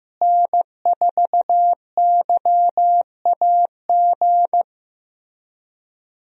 Morse code, heard as a keyed tone, about 15 wpm, 700 Hz. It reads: N4YAG
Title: USA Call Signs - 5 Characters - Repeated 3x - 15wpm